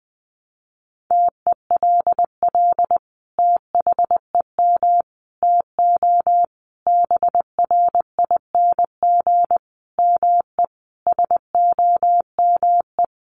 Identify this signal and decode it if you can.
Morse code, heard as a keyed tone, about 20 wpm, 700 Hz. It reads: TELL THEM TO BRING ME SOME
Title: Sentences from Top 1200 Words 20wpm